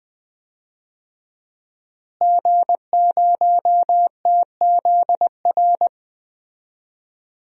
Morse code, heard as a keyed tone, about 20 wpm, 700 Hz. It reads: G0TZR
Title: DXCC-100 Call Signs - 5 Characters - Repeated 2x - 20wpm